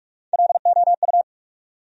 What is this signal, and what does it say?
Morse code, heard as a keyed tone, about 46 wpm, 700 Hz. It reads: LOU
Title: USA Common Names - Max 3 Letters - 45wpm